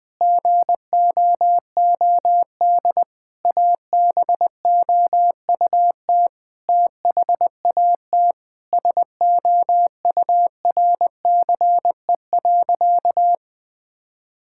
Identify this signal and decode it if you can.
Morse code, heard as a keyed tone, about 20 wpm, 700 Hz. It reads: GOOD ABOUT THAT SOURCE.
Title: Sentences from Top 1000 Words - Review Part 5 - 20wpm